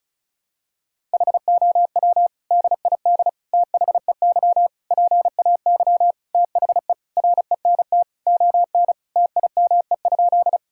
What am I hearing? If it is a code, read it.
Morse code, heard as a keyed tone, about 34 wpm, 700 Hz. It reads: HOW DID THEY PAY THE RENT ON TIME?
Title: Sentences from Top 1000 Words - Review Part 5 - 35wpm